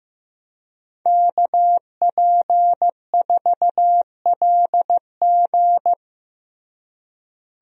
Morse code, heard as a keyed tone, about 15 wpm, 700 Hz. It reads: KP4LG
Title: DXCC-100 Call Signs - 5 Characters - Repeated 2x - 15wpm